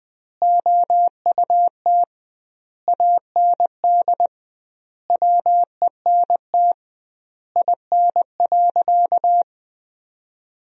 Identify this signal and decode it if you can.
Morse code, heard as a keyed tone, about 20 wpm, 700 Hz. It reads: OUT AND WENT IN.
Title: 2x Word-Spacing - Sentences from Top 300 Words - 20wpm